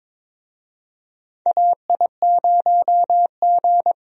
Morse code, heard as a keyed tone, about 21 wpm, 700 Hz. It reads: AI0G
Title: USA Call Signs - 4 Characters - Repeated 3x - 22wpm